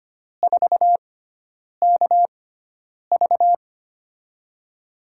Morse code, heard as a keyed tone, about 25 wpm, spaced as for 9 wpm, 700 Hz. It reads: 4KV
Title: USA Call Signs - 4 Characters - 6x Character Spacing - 25wpm